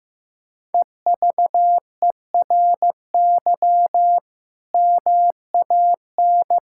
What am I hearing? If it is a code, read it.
Morse code, heard as a keyed tone, about 15 wpm, 700 Hz. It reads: EVERY MAN
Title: Sentences from Top 1000 Words - Review Part 6 - 15wpm